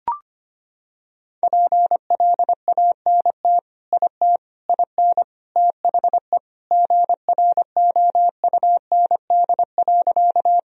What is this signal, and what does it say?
Morse code, heard as a keyed tone, about 25 wpm, 700 Hz. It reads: PLANT IT IN THE GROUND.
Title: Sentences from Top 1000 Words - Review Part 1 - 25wpm